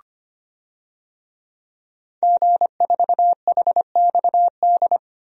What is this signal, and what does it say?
Morse code, heard as a keyed tone, about 25 wpm, 700 Hz. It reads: G4HXD
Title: DXCC-100 Call Signs - 5 Characters - Repeated 2x - 25wpm